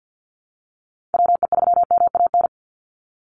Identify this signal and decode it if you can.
Morse code, heard as a keyed tone, about 49 wpm, 700 Hz. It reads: WE3MAN
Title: USA Call Signs 50wpm